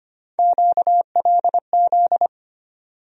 Morse code, heard as a keyed tone, about 25 wpm, 700 Hz. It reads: QLZ
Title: USA Call Sign Postfixes - 3 Letters 25wpm